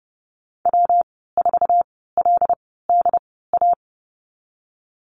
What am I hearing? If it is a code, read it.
Morse code, heard as a keyed tone, about 30 wpm, spaced as for 17 wpm, 700 Hz. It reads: W4LDA
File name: Lesson 19g - Letter w - Call Signs 30wpm